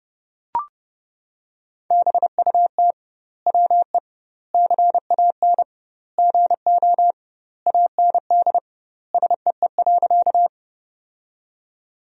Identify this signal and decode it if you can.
Morse code, heard as a keyed tone, about 30 wpm, 700 Hz. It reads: BUT WE CAN GO AND SEE.
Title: 2x Word-Spacing - Sentences from Top 100 Words - 30wpm